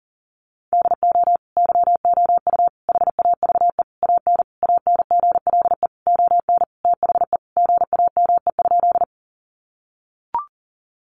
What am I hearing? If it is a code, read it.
Morse code, heard as a keyed tone, about 40 wpm, 700 Hz. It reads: DO YOU HAVE AN ANGLE ON THE GAME?
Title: Sentences from Top 1500 Words 40wpm